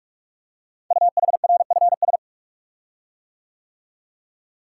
Morse code, heard as a keyed tone, about 44 wpm, 690 Hz. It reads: UHRFS
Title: Code Groups - 5 Letters - 45wpm